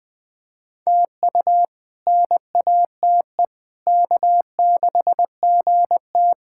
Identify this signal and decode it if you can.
Morse code, heard as a keyed tone, about 21 wpm, 700 Hz. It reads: TU NATE K6GT
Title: K1USN SST QSO Practice - 20wpm